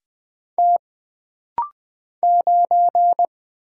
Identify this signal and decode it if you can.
Morse code, heard as a keyed tone, about 20 wpm, 700 Hz. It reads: T 9
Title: Single Letter-Number - Rapid-Fire 20wpm